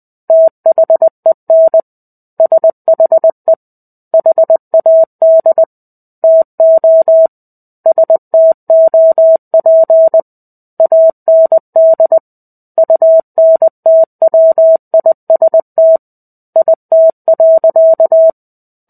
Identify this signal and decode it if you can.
Morse code, heard as a keyed tone, about 20 wpm, 650 Hz. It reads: THEN SHE HAD TO STOP AND UNTWIST IT.
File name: 05-Chapter-05 20 WPM